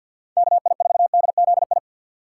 Morse code, heard as a keyed tone, about 52 wpm, 700 Hz. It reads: KI4DZI
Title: USA Call Signs - 6 Characters - 6x Character Spacing - 50wpm